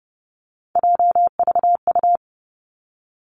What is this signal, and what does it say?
Morse code, heard as a keyed tone, about 30 wpm, 700 Hz. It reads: JVU
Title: Three Challenging Letter-Number Combinations 30wpm